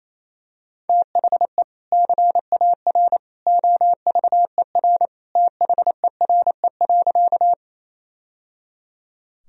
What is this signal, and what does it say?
Morse code, heard as a keyed tone, about 27 wpm, 700 Hz. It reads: THE CAR OVER THERE.
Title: Sentences from Top 1000 Words - Review Part 1 - 28wpm